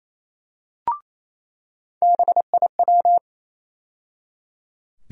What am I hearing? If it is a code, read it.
Morse code, heard as a keyed tone, about 28 wpm, 700 Hz. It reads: BIW
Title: USA Call Sign Postfixes - 3 Letters 28wpm